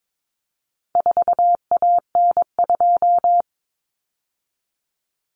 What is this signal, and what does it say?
Morse code, heard as a keyed tone, about 22 wpm, 700 Hz. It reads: 4AN2
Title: Common Cut Numbers - 4-Digit 22wpm